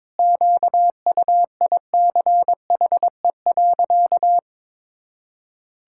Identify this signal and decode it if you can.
Morse code, heard as a keyed tone, about 22 wpm, 700 Hz. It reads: QUICHE.
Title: Pangram Challenge v2 - 22wpm